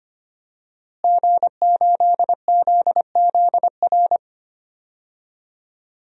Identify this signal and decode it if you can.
Morse code, heard as a keyed tone, about 25 wpm, 700 Hz. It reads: G8ZZR
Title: DXCC-100 Call Signs - 5 Characters - Repeated 3x - 25wpm